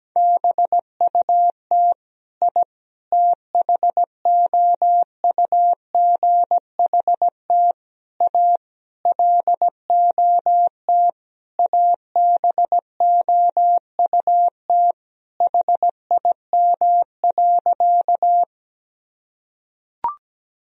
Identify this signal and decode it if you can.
Morse code, heard as a keyed tone, about 17 wpm, 700 Hz. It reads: BUT I THOUGHT A LOT ABOUT HIM.
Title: Sentences from Top 1000 Words - Review Part 3 - 17wpm